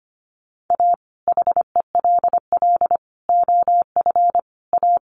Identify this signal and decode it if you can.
Morse code, heard as a keyed tone, about 25 wpm, 700 Hz. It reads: A HELL OF A
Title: Sets of 5 Words Encore 25wpm